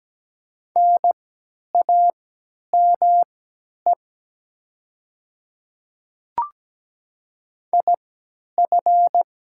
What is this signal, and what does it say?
Morse code, heard as a keyed tone, about 17 wpm, spaced as for 10 wpm, 700 Hz. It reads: NAME IF
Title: 3x Character Spacing - Top 200 Words - 17wpm